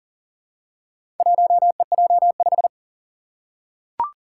Morse code, heard as a keyed tone, about 37 wpm, 700 Hz. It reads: 1EJ5
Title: Four Challenging Letter-Number Combinations 40wpm